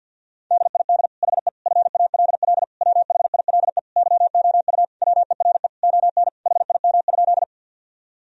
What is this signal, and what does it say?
Morse code, heard as a keyed tone, about 52 wpm, 700 Hz. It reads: DID HE FALL WHILE YOU WERE ON HIM?
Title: Sentences from Top 1000 Words - Review Part 2 - 50wpm